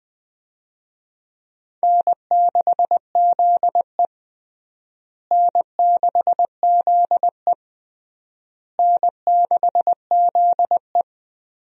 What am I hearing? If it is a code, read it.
Morse code, heard as a keyed tone, about 21 wpm, 700 Hz. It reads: N6ZE N6ZE N6ZE
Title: USA Call Signs - 4 Characters - Repeated 3x - 20wpm